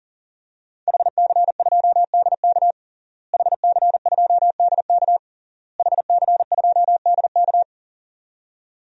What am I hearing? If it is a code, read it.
Morse code, heard as a keyed tone, about 40 wpm, 700 Hz. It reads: HC2DK HC2DK HC2DK
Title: DXCC-100 Call Signs - 5 Characters - Repeated 3x - 40wpm